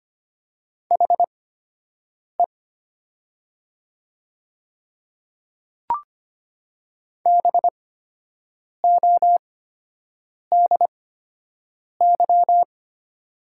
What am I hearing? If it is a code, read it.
Morse code, heard as a keyed tone, about 25 wpm, spaced as for 7 wpm, 700 Hz. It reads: HE BODY
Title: 8x Character Spacing - Top 500 Words - 25wpm